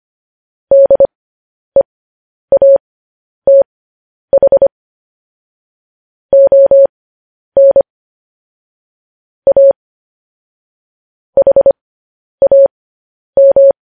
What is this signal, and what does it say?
Morse code, heard as a keyed tone, about 25 wpm, spaced as for 10 wpm, 550 Hz. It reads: DEATH ON A HAM
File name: podcast / News headlines-1x-25x10